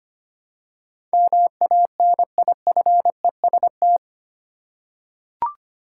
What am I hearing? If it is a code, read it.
Morse code, heard as a keyed tone, about 25 wpm, 700 Hz. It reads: MANIFEST
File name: Eight Letter Words - 25wpm